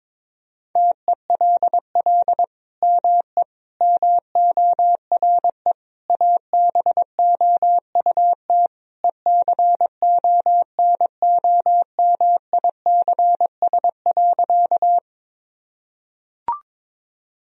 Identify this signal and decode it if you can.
Morse code, heard as a keyed tone, about 22 wpm, 700 Hz. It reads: TELL ME MORE ABOUT ECONOMICS.
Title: Sentences from Top 1000 Words - Review Part 3 - 22wpm